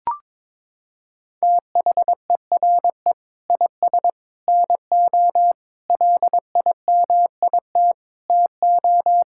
Morse code, heard as a keyed tone, about 22 wpm, 700 Hz. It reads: THERE IS NO LIMIT TO
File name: practice / 22wpm / Sentences with 8-Word Phrases - 22wpm